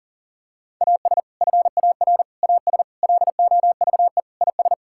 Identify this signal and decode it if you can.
Morse code, heard as a keyed tone, about 40 wpm, 700 Hz. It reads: AS FAR AS LOVE IS
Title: Sentences from Top 1000 Words - Review Part 6 - 40wpm